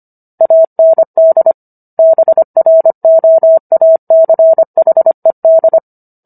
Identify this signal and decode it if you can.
Morse code, heard as a keyed tone, about 25 wpm, 650 Hz. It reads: AND BROACHED